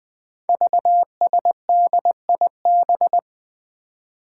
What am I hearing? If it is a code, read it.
Morse code, heard as a keyed tone, about 20 wpm, 700 Hz. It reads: VSDIB
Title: Five Challenging Letter-Number Combinations 20wpm